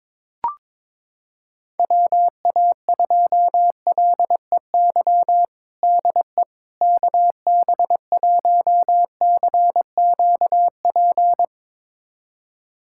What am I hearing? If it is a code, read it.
Morse code, heard as a keyed tone, about 21 wpm, 700 Hz. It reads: WA2LEY DE KB1CQP
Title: USA Callsign DE Callsign 22wpm